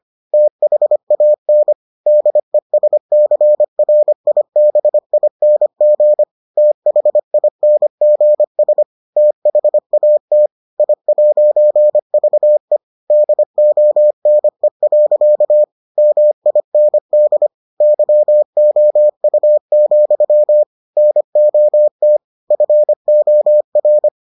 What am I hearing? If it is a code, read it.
Morse code, heard as a keyed tone, about 25 wpm, 600 Hz. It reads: THAN DESCRIBING THINGS THAT I'VE DONE. MIND YOU, NOT FOR